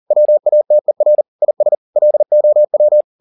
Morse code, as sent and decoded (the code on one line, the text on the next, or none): .-- .- - . .-. / .. ... / .-.. --- .--
WATER IS LOW